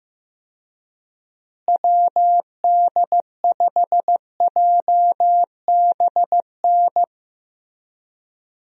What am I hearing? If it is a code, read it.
Morse code, heard as a keyed tone, about 15 wpm, 700 Hz. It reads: WD5JBN
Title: USA Call Signs - 6 Characters - Repeated 2x - 15wpm